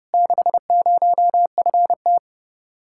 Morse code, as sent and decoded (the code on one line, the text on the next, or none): -.... ----- ..-. -
60FT